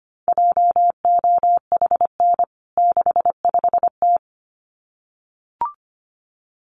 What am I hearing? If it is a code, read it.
Morse code, heard as a keyed tone, about 25 wpm, 700 Hz. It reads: JOHN 65T
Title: CWT Exchange 25wpm